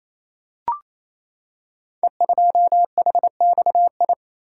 Morse code, heard as a keyed tone, about 27 wpm, 700 Hz. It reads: E2HXI
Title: Code Groups - 5 Letter:Numbers - 28wpm